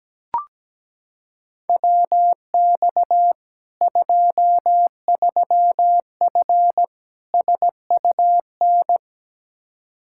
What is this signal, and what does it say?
Morse code, heard as a keyed tone, about 17 wpm, 700 Hz. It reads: WX 23F SUN
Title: Practice QSO - Exchange 2 - Info - 17wpm